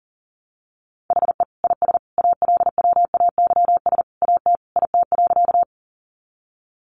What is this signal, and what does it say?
Morse code, heard as a keyed tone, about 40 wpm, 700 Hz. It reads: HE IS ALWAYS AT IT.
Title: Sentences from Top 300 Words 40wpm